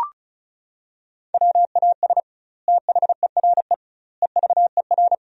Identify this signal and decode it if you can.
Morse code, heard as a keyed tone, about 36 wpm, 700 Hz. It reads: WAS THERE EVER
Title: 2x Word-Spacing - Sentences from Top 400 Words - 35wpm